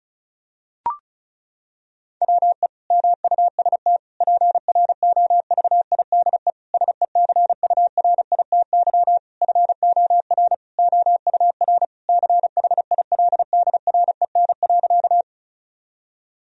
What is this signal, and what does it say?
Morse code, heard as a keyed tone, about 36 wpm, 700 Hz. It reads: WE MUST PROVIDE SECURITY FOR OUR CHILDREN.
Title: Sentences from Top 1000 Words - Review Part 4 - 35wpm